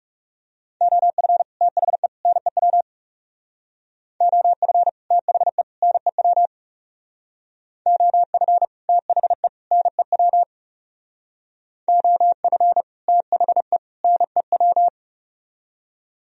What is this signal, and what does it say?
Morse code, heard as a keyed tone, about 35 wpm, 700 Hz. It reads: OF THE NEW OF THE NEW OF THE NEW OF THE NEW
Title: Speed-Racing - Sets of 3 Words 30wpm